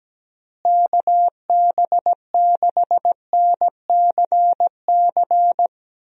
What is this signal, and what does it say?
Morse code, heard as a keyed tone, about 17 wpm, 700 Hz. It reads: KB6NCC
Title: USA Call Signs - 6 Characters - Repeated 2x - 17wpm